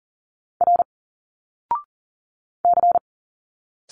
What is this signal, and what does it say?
Morse code, heard as a keyed tone, about 40 wpm, 700 Hz. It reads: R C
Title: Single Letters 40wpm